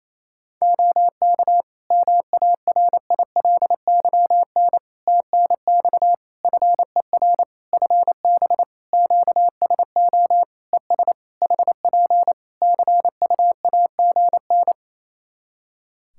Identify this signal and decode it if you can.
Morse code, heard as a keyed tone, about 28 wpm, 700 Hz. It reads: OK MARILYN TNX FER FB QSO ES HP CUAGN
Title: Practice QSO - Exchange 4 - Ending - 28wpm